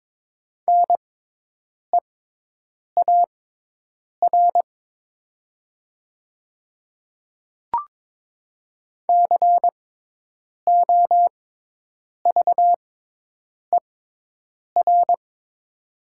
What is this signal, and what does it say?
Morse code, heard as a keyed tone, about 22 wpm, spaced as for 8 wpm, 700 Hz. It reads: NEAR COVER
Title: 6x Character Spacing - Top 300 Words - 22wpm